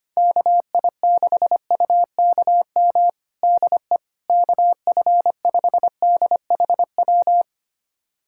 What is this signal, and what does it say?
Morse code, heard as a keyed tone, about 25 wpm, 700 Hz. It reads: KI6UKM DE KF5DHW
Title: USA Callsign DE Callsign - 6 Characters - 25wpm